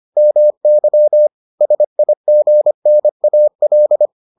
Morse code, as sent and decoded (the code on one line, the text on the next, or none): -- -.-- / ... .. --. -. .- .-..
MY SIGNAL